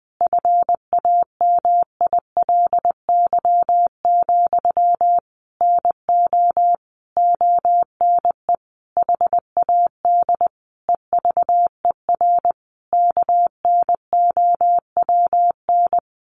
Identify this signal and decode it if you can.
Morse code, heard as a keyed tone, about 20 wpm, 700 Hz. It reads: FAMILY, NO ONE HAD EVER KNOWN
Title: Sentences from Top 1200 Words 20wpm